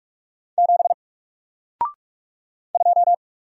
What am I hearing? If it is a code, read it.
Morse code, heard as a keyed tone, about 44 wpm, 700 Hz. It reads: 7 2
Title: Single Character (Letter:Number:Punctuation:Prosign) 45wpm